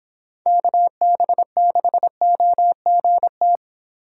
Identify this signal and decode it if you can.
Morse code, heard as a keyed tone, about 26 wpm, 700 Hz. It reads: KB6OGT